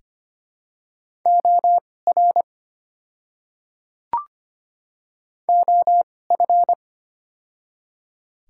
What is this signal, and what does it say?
Morse code, heard as a keyed tone, about 25 wpm, spaced as for 18 wpm, 700 Hz. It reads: OR OF
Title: 2x Character Spacing - Top 100 Words - 25wpm